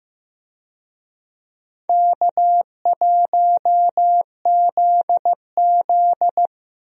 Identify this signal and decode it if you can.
Morse code, heard as a keyed tone, about 15 wpm, 700 Hz. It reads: K1ZZ
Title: USA Call Signs - 4 Characters - Repeated 3x - 15wpm